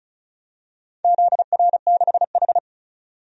Morse code, heard as a keyed tone, about 35 wpm, 700 Hz. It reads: ZR6H
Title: DXCC-100 Call Signs - 4 Characters - 35wpm